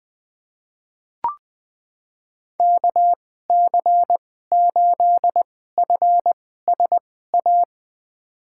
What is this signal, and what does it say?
Morse code, heard as a keyed tone, about 20 wpm, spaced as for 14 wpm, 700 Hz. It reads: KC8FSA
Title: USA Call Signs - 6 Characters - 2x Character Spacing - 20wpm